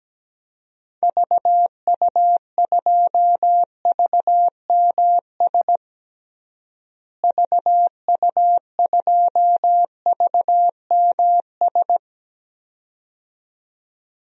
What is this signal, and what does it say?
Morse code, heard as a keyed tone, about 17 wpm, 700 Hz. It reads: VU2VMS VU2VMS
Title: DXCC-100 Call Signs - 6 Characters - Repeated 2x - 17wpm